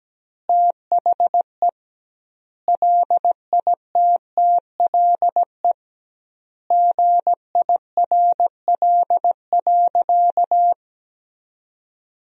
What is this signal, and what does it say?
Morse code, heard as a keyed tone, about 17 wpm, 700 Hz. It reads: THE LITTLE GIRL.
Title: 2x Word-Spacing - Sentences from Top 400 Words - 17wpm